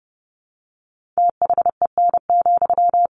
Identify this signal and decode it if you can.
Morse code, heard as a keyed tone, about 30 wpm, 700 Hz. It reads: THEN,
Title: Sentences from Top 1500 Words 30wpm